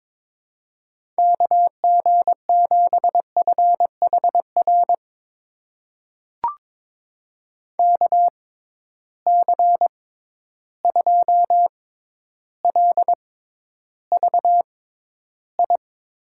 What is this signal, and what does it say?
Morse code, heard as a keyed tone, about 22 wpm, spaced as for 11 wpm, 700 Hz. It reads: KG7FHR KC2LVI
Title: USA Call Signs - 6 Characters - 6x Character Spacing - 22wpm